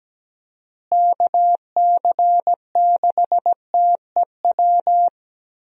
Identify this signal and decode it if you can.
Morse code, heard as a keyed tone, about 17 wpm, 700 Hz. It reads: KC6TEW
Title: USA Call Signs - 6 Characters - Repeated 3x - 17wpm